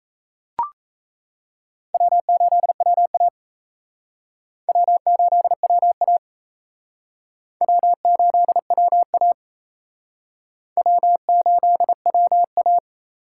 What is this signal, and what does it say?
Morse code, heard as a keyed tone, about 34 wpm, 700 Hz. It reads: W8WA W8WA W8WA W8WA
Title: Speed Racing - USA Call Signs 28wpm